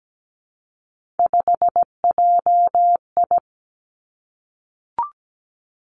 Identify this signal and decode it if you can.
Morse code, heard as a keyed tone, about 17 wpm, 700 Hz. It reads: 5JI
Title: Three Challenging Letter-Number Combinations 17wpm